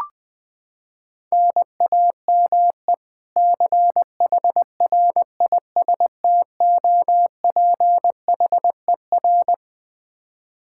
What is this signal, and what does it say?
Morse code, heard as a keyed tone, about 20 wpm, 700 Hz. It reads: NAME CHRISTOPHER
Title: Practice QSO - Exchange 1 - Beginning - 20wpm